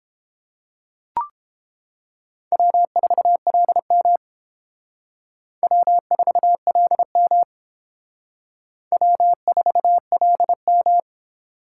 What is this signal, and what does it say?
Morse code, heard as a keyed tone, about 28 wpm, 700 Hz. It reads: W4LM W4LM W4LM